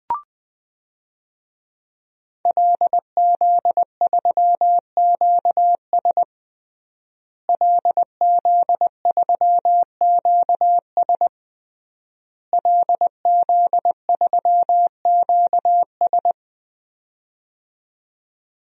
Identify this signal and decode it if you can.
Morse code, heard as a keyed tone, about 20 wpm, 700 Hz. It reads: LZ3QS LZ3QS LZ3QS
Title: DXCC-100 Call Signs - 5 Characters - Repeated 3x - 20wpm